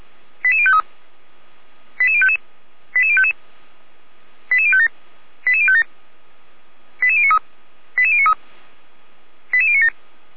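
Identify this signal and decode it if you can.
Five-tone selective call (ZVEI), heard as quick runs of five tones, tones of about 70 ms. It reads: 88053 88055 88055 88057 88057 88093 88093 88097
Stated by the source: ZVEI five-tone selective call received on SDR, from the Feuerwehr Orlamünde audio server.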